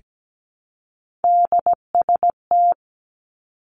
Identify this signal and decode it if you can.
Morse code, heard as a keyed tone, about 17 wpm, 700 Hz. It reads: DST
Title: Three Challenging Letter-Number Combinations 17wpm